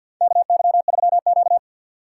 Morse code, heard as a keyed tone, about 51 wpm, 700 Hz. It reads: KY3Q
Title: USA Call Signs - 4 Characters - 3x Character Spacing - 50wpm